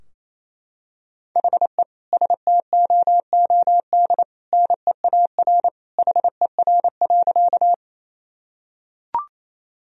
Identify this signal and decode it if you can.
Morse code, heard as a keyed tone, about 27 wpm, 700 Hz. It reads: HE STOOD NEAR HER.
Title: Sentences from Top 1000 Words - Review Part 2 - 28wpm